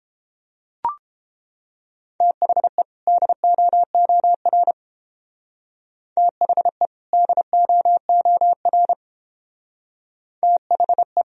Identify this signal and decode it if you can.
Morse code, heard as a keyed tone, about 31 wpm, 700 Hz. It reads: THE DOOR THE DOOR THE